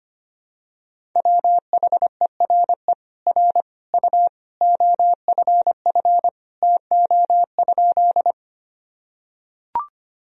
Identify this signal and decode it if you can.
Morse code, heard as a keyed tone, about 25 wpm, 700 Hz. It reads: WHERE R U OFF TO?
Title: Sentences from Top 200 Words and Abbreviations 25wpm